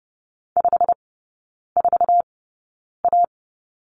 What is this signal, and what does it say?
Morse code, heard as a keyed tone, about 30 wpm, spaced as for 9 wpm, 700 Hz. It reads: 54A
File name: Lesson 17e - Letter u - Three Characters 30wpm